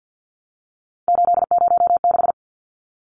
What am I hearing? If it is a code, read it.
Morse code, heard as a keyed tone, about 50 wpm, 700 Hz. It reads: 806
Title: Numbers - 3-Digit 50wpm